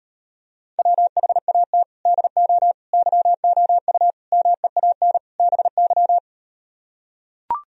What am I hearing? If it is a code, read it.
Morse code, heard as a keyed tone, about 38 wpm, 700 Hz. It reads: WHAT DO YOU MEAN BY